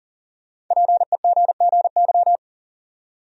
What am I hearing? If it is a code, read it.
Morse code, heard as a keyed tone, about 40 wpm, 700 Hz. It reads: PEGGY
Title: USA Common Names - Max 5 Letters - 40wpm